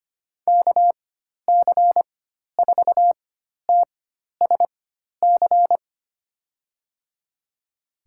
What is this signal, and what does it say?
Morse code, heard as a keyed tone, about 25 wpm, spaced as for 12 wpm, 700 Hz. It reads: KC4TSC
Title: USA Call Signs - 6 Characters - 4x Character Spacing - 25wpm